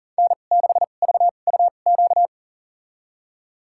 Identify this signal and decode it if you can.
Morse code, heard as a keyed tone, about 40 wpm, spaced as for 29 wpm, 700 Hz. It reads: N6VUQ
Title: USA Call Signs - 2x Character Spacing - 40wpm